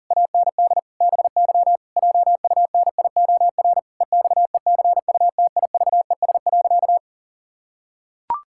Morse code, heard as a keyed tone, about 40 wpm, 700 Hz. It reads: AND BY JUNIOR EXECUTIVES.
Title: Pangram Challenge v2 - 40wpm